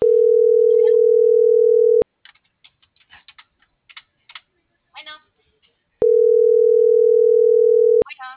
Ringback tone (North America), 440 + 480 Hz.